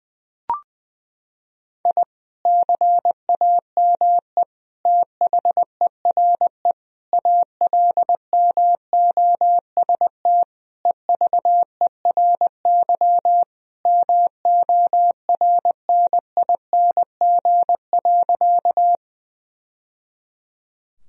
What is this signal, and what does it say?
Morse code, heard as a keyed tone, about 20 wpm, 700 Hz. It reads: I CAME THERE ALMOST EVERY MORNING.
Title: Sentences from Top 1000 Words - Review Part 4 - 20wpm